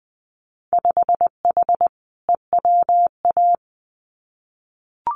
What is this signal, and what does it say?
Morse code, heard as a keyed tone, about 20 wpm, 700 Hz. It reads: 5H EWA
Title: Winter Field Day Exchanges Encore 20wpm